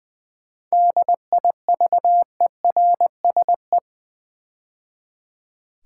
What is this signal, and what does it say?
Morse code, heard as a keyed tone, about 20 wpm, 700 Hz. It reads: DIVERSE
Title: 7 Letter Words - First 3 Letters Given - 20wpm